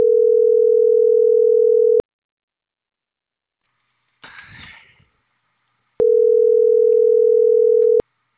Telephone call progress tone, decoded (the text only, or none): ringback tone